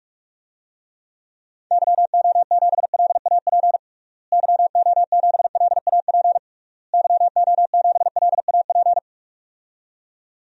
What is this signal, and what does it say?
Morse code, heard as a keyed tone, about 42 wpm, 700 Hz. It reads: YO7LAP YO7LAP YO7LAP
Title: DXCC-100 Call Signs - 6 Characters - Repeated 3x - 45wpm